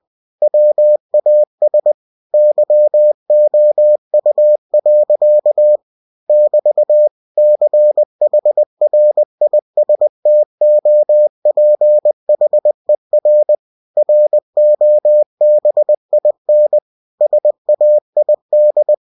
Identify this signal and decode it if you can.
Morse code, heard as a keyed tone, about 20 wpm, 600 Hz. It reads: WAS YOU. <BT> CHRISTOPHER ROBIN SAID
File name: cw-books / winnie-the-pooh / winnie-20wpm-page-0004